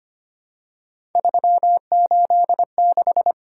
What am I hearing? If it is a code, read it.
Morse code, heard as a keyed tone, about 25 wpm, 700 Hz. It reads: 386
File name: Three Letter-Number Combinations - 25wpm